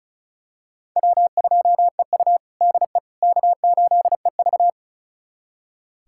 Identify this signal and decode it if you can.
Morse code, heard as a keyed tone, about 34 wpm, 700 Hz. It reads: W2EU DE K8EV
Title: USA Callsign DE Callsign - 4 Characters - 35wpm